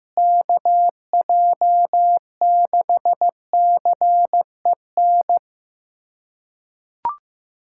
Morse code, heard as a keyed tone, about 15 wpm, 700 Hz. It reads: KJ6CEN
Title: USA Callsign DE Callsign - 6 Characters - 15wpm